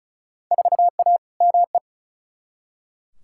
Morse code, heard as a keyed tone, about 34 wpm, 700 Hz. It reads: 4A ME
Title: Field Day Exchanges - 35wpm